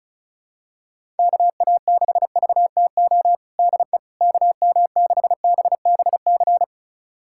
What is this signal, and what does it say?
Morse code, heard as a keyed tone, about 34 wpm, 700 Hz. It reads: KA6VTO DE KM6BBC
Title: USA Callsign DE Callsign 35wpm